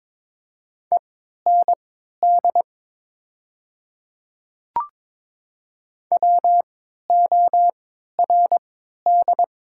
Morse code, heard as a keyed tone, about 22 wpm, spaced as for 13 wpm, 700 Hz. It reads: END WORD
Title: 3x Character Spacing - Top 200 Words - 22wpm